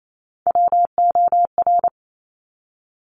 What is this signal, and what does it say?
Morse code, heard as a keyed tone, about 28 wpm, 700 Hz. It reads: WOR
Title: Top Three Letter Combinations in Words 28wpm